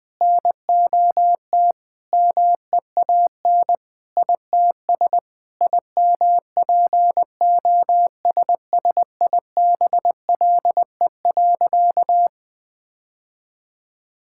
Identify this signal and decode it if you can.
Morse code, heard as a keyed tone, about 21 wpm, 700 Hz. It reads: NOT MEAN ITS IMPOSSIBLE.
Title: Sentences with 5-Word Phrases - 20wpm